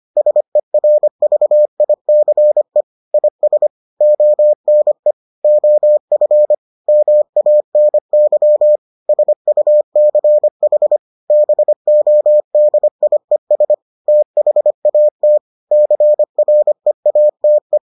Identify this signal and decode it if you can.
Morse code, heard as a keyed tone, about 25 wpm, 600 Hz. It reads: SERVICE IS ONE OF MANY SUCH BODIES THAT CREATE